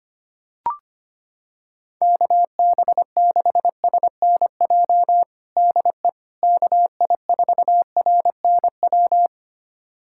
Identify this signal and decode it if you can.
Morse code, heard as a keyed tone, about 25 wpm, 700 Hz. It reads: KB6SNJ DE KI4RNW
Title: USA Callsign DE Callsign - 6 Characters - 25wpm